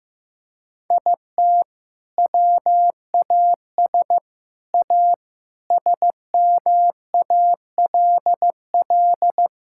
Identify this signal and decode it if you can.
Morse code, heard as a keyed tone, about 15 wpm, 700 Hz. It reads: IT WAS A SMALL
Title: Sentences from Top 1000 Words - Review Part 5 - 15wpm